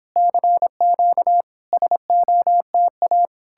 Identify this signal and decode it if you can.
Morse code, heard as a keyed tone, about 30 wpm, 700 Hz. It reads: CQ SOTA